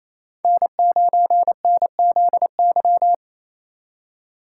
Morse code, heard as a keyed tone, about 28 wpm, 700 Hz. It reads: N9NZY
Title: USA Call Signs - 8x Character Spacing - 28wpm